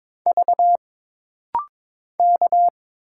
Morse code, heard as a keyed tone, about 22 wpm, 700 Hz. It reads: V K
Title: Single Letters - Rapid-Fire 22wpm